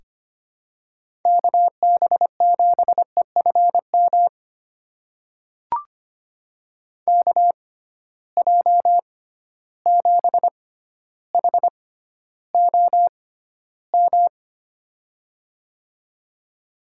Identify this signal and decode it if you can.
Morse code, heard as a keyed tone, about 25 wpm, spaced as for 13 wpm, 700 Hz. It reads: KB7EFM KJ7HOM
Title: USA Call Signs - 6x Character Spacing - 25wpm